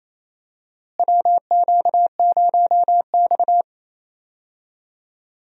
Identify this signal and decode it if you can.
Morse code, heard as a keyed tone, about 28 wpm, 700 Hz. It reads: WQ0X
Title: USA Call Signs - 4 Characters - Repeated 3x - 28wpm